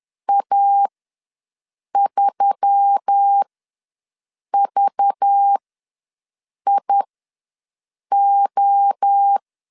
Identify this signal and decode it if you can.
Morse code, heard as a keyed tone, about 10 wpm, spaced as for 6 wpm, 800 Hz. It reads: A3VIO
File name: LEZIONI / 6 PKX0 / 6 PKX0 ws 10wpm 4220